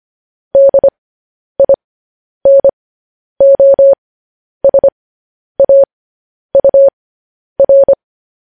Morse code, heard as a keyed tone, about 25 wpm, spaced as for 10 wpm, 550 Hz. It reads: DINOSAUR
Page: news-headlines-in-morse-code-at-10-wpm